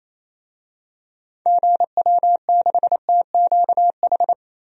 Morse code, heard as a keyed tone, about 27 wpm, 700 Hz. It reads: GW6TQH
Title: DXCC-100 Call Signs - 6 Characters - Repeated 2x - 28wpm